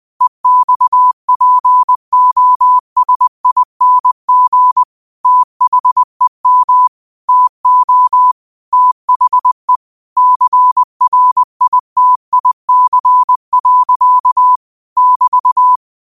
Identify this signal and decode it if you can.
Morse code, heard as a keyed tone, about 20 wpm, 1000 Hz. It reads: EXPOSING THEM TO THE CRITIC. <BT>
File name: qotdfiles / 20wpm